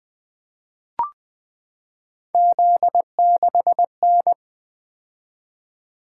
Morse code, heard as a keyed tone, about 20 wpm, 700 Hz. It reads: Z6N
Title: Three Letter-Number Combinations - 20wpm